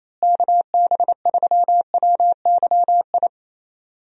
Morse code, heard as a keyed tone, about 28 wpm, 700 Hz. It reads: KB3WYI
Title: USA Call Signs - 6 Characters - Repeated 3x - 28wpm